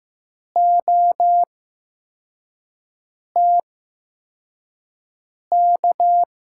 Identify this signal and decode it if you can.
Morse code, heard as a keyed tone, about 15 wpm, spaced as for 4 wpm, 700 Hz. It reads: OTK
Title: USA Call Signs - 6 Characters - 8x Character Spacing - 15wpm